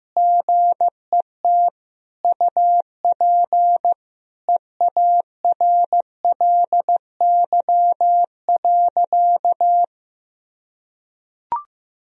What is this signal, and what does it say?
Morse code, heard as a keyed tone, about 15 wpm, 700 Hz. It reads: GET UP EARLY.
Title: Sentences from Top 1000 Words - Review Part 2 - 15wpm